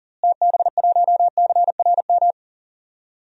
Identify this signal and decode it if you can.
Morse code, heard as a keyed tone, about 40 wpm, 700 Hz. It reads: TB1CRM
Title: DXCC-100 Call Signs - 6 Characters - Repeated 3x - 40wpm